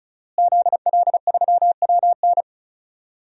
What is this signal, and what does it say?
Morse code, heard as a keyed tone, about 36 wpm, 700 Hz. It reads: ZL3WN
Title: DXCC-100 Call Signs - 5 Characters - Repeated 3x - 35wpm